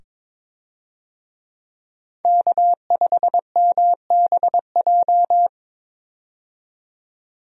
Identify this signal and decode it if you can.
Morse code, heard as a keyed tone, about 22 wpm, 700 Hz. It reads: K5MBJ
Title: USA Call Signs - 5 Characters - 22wpm